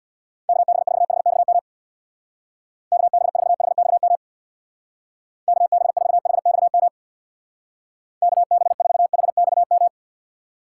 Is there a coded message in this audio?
Yes: Morse code from 0.5 to 9.9 s, heard as a keyed tone, about 66 wpm, 690 Hz. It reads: KB4HXM KB4HXM KB4HXM KB4HXM